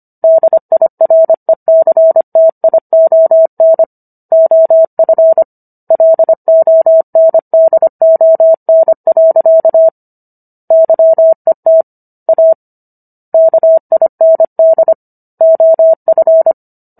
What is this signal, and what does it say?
Morse code, heard as a keyed tone, about 25 wpm, 650 Hz. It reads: DIRECTION OF LONDON. YET A KIND OF